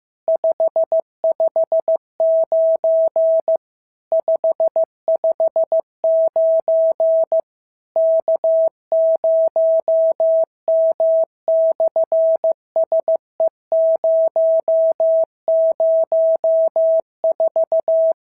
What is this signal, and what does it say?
Morse code, heard as a keyed tone, about 16 wpm, 650 Hz. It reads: 559 559 K0M/SE004
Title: SOTA QSO Practice 15wpm